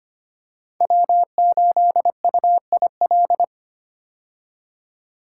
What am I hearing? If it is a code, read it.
Morse code, heard as a keyed tone, about 25 wpm, 700 Hz. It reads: W8UIL
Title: K1USN SST QSO Practice - 25wpm